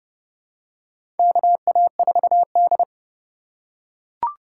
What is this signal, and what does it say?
Morse code, heard as a keyed tone, about 30 wpm, 700 Hz. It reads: KA4D
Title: USA Call Signs - 4 Characters - 3x Character Spacing - 30wpm